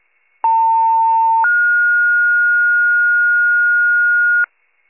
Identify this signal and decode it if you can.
Two-tone page, heard as a one-second tone, then a three-second tone, about 900 then 1430 Hz, with noise. It is heard from 0.4 to 4.4 s.